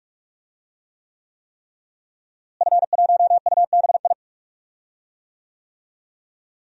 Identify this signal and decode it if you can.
Morse code, heard as a keyed tone, about 42 wpm, 700 Hz. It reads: F1UBI
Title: DXCC-100 Call Signs - 5 Characters - Repeated 3x - 45wpm